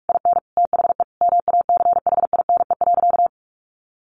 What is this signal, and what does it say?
Morse code, heard as a keyed tone, about 44 wpm, 700 Hz. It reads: IN THE MACHINE.
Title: A Sound Of Thunder 45wpm